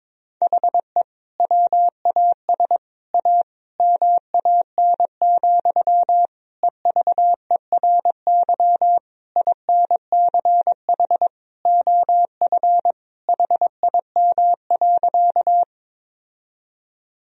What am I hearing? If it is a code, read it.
Morse code, heard as a keyed tone, about 21 wpm, 700 Hz. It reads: HE WAS A MAN, EVERY INCH OF HIM.
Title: Sentences from Top 1000 Words - Review Part 3 - 22wpm